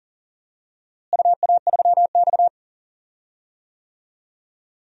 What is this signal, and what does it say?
Morse code, heard as a keyed tone, about 37 wpm, 700 Hz. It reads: UA3X
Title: DXCC-100 Call Signs - 4 Characters - Repeated 2x - 40wpm